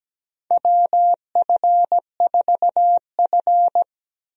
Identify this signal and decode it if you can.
Morse code, heard as a keyed tone, about 17 wpm, 700 Hz. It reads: WF4F
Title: USA Call Signs - 4 Characters - Repeated 3x - 17wpm